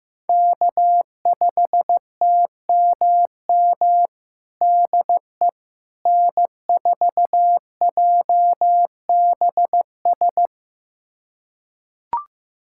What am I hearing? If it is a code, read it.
Morse code, heard as a keyed tone, about 15 wpm, 700 Hz. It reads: K5TMM DE N4JBS
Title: USA Callsign DE Callsign - 5 Characters - 15wpm